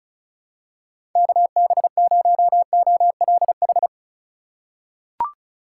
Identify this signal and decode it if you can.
Morse code, heard as a keyed tone, about 34 wpm, 700 Hz. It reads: KB0OLH
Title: USA Call Signs - 6 Characters - 4x Character Spacing - 35wpm